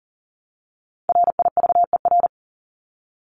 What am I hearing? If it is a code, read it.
Morse code, heard as a keyed tone, about 40 wpm, 700 Hz. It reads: RIVER
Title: Top 500 Words 40wpm